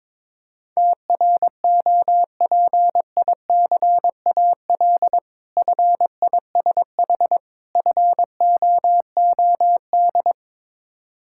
Morse code, heard as a keyed tone, about 21 wpm, 690 Hz. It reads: TROPICAL FISH FOOD
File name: Difficult Words with Hints - 22wpm